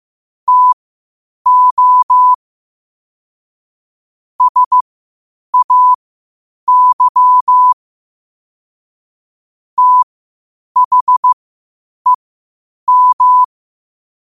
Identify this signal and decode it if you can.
Morse code, heard as a keyed tone, about 15 wpm, spaced as for 8 wpm, 1000 Hz. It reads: TO SAY THEM
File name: qotdfiles / 8wpm